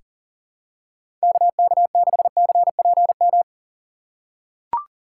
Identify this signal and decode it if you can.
Morse code, heard as a keyed tone, about 39 wpm, 700 Hz. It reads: KK6CPM
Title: USA Call Signs - 6 Characters - 2x Character Spacing - 40wpm